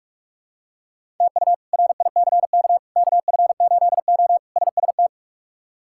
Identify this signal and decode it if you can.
Morse code, heard as a keyed tone, about 43 wpm, 700 Hz. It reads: TU RICK KF8O SST
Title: K1USN SST QSO Practice - 45wpm